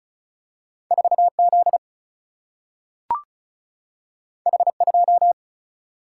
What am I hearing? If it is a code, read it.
Morse code, heard as a keyed tone, about 36 wpm, 700 Hz. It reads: 4 Z H 2
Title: Single-Letter Single-Number 35wpm